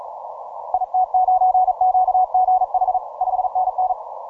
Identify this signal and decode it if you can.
Morse code, heard as a keyed tone, about 38 wpm, 740 Hz. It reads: IT9QGH 5NN
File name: simple-file-list / C92RU 12MCW